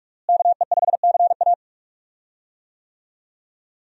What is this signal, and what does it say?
Morse code, heard as a keyed tone, about 46 wpm, 700 Hz. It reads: KE5CA
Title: USA Call Signs - 5 Characters - Repeated 2x - 45wpm